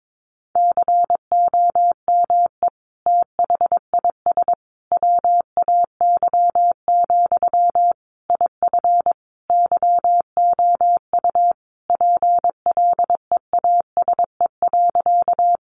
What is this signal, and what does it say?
Morse code, heard as a keyed tone, about 22 wpm, 700 Hz. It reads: COME THIS WAY, IF YOU PLEASE.
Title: Sentences from Top 1500 Words 22wpm